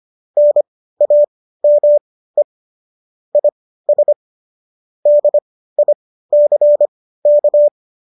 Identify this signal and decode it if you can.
Morse code, heard as a keyed tone, about 25 wpm, spaced as for 15 wpm, 590 Hz. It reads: NAME IS DICK